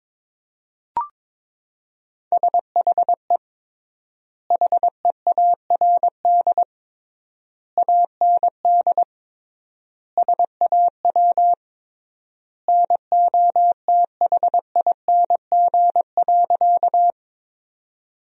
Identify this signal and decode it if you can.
Morse code, heard as a keyed tone, about 21 wpm, 700 Hz. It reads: SHE HEARD AND SAW NOTHING.
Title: 3x Word-Spacing - Sentences from Top 500 Words - 22wpm